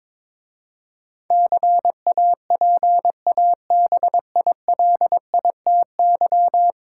Morse code, heard as a keyed tone, about 21 wpm, 700 Hz. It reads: CAPABILITY
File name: practice / 22wpm / 10 Letter Words - First 3 Letters Given - 22wpm